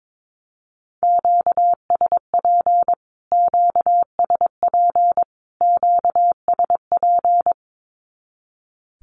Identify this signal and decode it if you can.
Morse code, heard as a keyed tone, about 22 wpm, 700 Hz. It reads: QSP QSP QSP
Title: Q-Codes Repeated 3x 22wpm